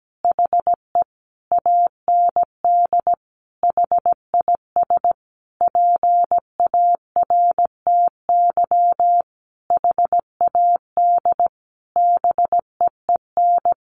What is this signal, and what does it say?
Morse code, heard as a keyed tone, about 17 wpm, 700 Hz. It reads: HE AND HIS PARTY HAD BEEN
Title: Sentences from Top 1400 Words 17wpm